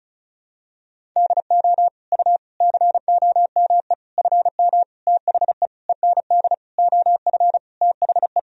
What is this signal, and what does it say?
Morse code, heard as a keyed tone, about 36 wpm, 700 Hz. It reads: DO U COME FM THE END OF THE
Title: Sentences from Top 200 Words and Abbreviations 35wpm